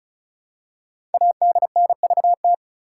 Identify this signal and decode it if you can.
Morse code, heard as a keyed tone, about 36 wpm, 700 Hz. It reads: ADNVT
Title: Code Groups - 5 Letters - 35wpm